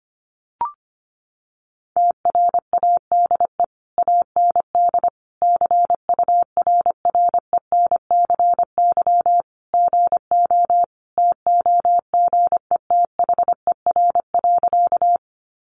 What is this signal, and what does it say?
Morse code, heard as a keyed tone, about 25 wpm, 700 Hz. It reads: TRADE AND CURRENCY GO TOGETHER.
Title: Sentences from Top 1500 Words 25wpm